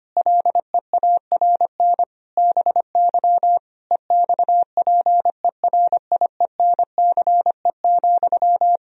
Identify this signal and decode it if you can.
Morse code, heard as a keyed tone, about 25 wpm, 700 Hz. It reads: LEARN BY EXPERIENCE,
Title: Sentences from Top 1000 Words - Review Part 3 - 25wpm